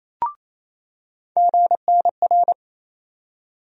Morse code, heard as a keyed tone, about 28 wpm, 700 Hz. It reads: GNR
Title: USA Call Sign Postfixes - 3 Letters 28wpm